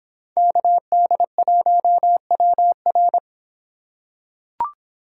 Morse code, heard as a keyed tone, about 26 wpm, 700 Hz. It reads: KD1WR